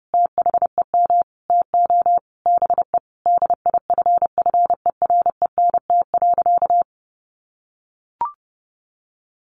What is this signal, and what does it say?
Morse code, heard as a keyed tone, about 30 wpm, 700 Hz. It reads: THEM TO BE DIFFERENT.
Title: Sentences from Top 1500 Words 30wpm